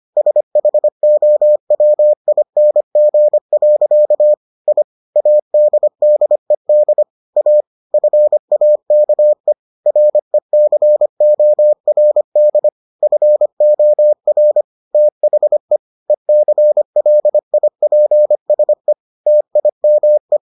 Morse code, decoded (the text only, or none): SHOWING. I ADDED A FAKE RECORD FOR THE ECLIPSE TIME